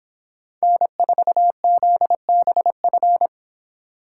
Morse code, heard as a keyed tone, about 26 wpm, 700 Hz. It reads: N4ZBF